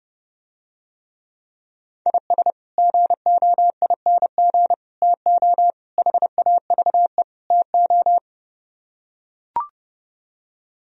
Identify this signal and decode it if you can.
Morse code, heard as a keyed tone, about 30 wpm, 700 Hz. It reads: IS GOING TO HAVE TO